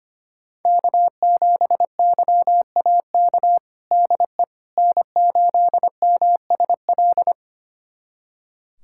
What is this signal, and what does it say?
Morse code, heard as a keyed tone, about 25 wpm, 700 Hz. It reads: K7YAK DE N8MSL
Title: USA Callsign DE Callsign - 5 Characters - 25wpm